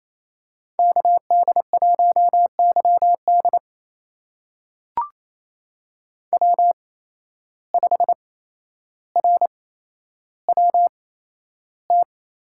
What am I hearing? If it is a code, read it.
Morse code, heard as a keyed tone, about 28 wpm, spaced as for 12 wpm, 700 Hz. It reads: KD1YD W5RWT
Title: USA Call Signs - 5 Characters - 8x Character Spacing - 28wpm